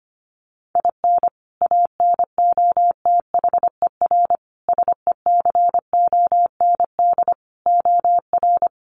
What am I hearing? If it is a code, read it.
Morse code, heard as a keyed tone, about 25 wpm, 700 Hz. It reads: IN ANOTHER SECOND OR
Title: Sentences from Top 1500 Words 25wpm